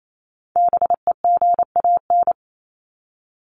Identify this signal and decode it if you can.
Morse code, heard as a keyed tone, about 28 wpm, 700 Hz. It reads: BEGAN
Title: Top 300 Words 28wpm